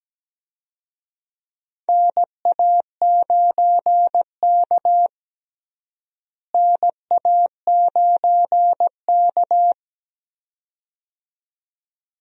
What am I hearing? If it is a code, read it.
Morse code, heard as a keyed tone, about 17 wpm, 700 Hz. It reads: NA9K NA9K
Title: USA Call Signs - 4 Characters - Repeated 2x - 17wpm